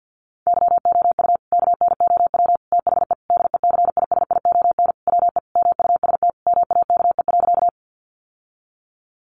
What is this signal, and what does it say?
Morse code, heard as a keyed tone, about 48 wpm, 700 Hz. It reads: YOU KNOW THE DECISION WE MUST MAKE.
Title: Sentences from Top 1000 Words 50wpm